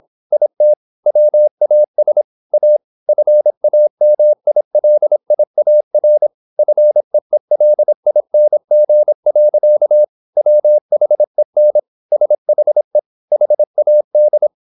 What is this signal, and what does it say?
Morse code, heard as a keyed tone, about 26 wpm, 600 Hz. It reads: IT WAS A FAMILIAR FEELING. WHEN SHE HAD